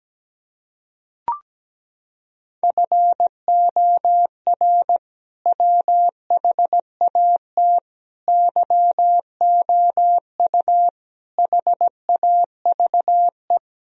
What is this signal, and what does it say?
Morse code, heard as a keyed tone, about 17 wpm, 700 Hz. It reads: FOR WHAT YOU HAVE
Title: Sentences from Top 1000 Words - Review Part 2 - 17wpm